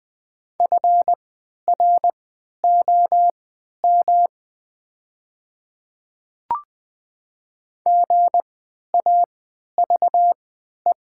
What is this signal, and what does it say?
Morse code, heard as a keyed tone, about 20 wpm, spaced as for 11 wpm, 700 Hz. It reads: FROM GAVE
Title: 3x Character Spacing - Top 500 Words - 20wpm